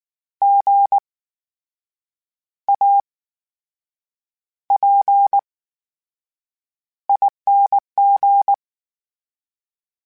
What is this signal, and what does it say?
Morse code, heard as a keyed tone, about 19 wpm, spaced as for 7 wpm, 800 Hz. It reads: GAPING
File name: Special INGs / Special INGs-mp3s / VerySlowToFaster Ing @18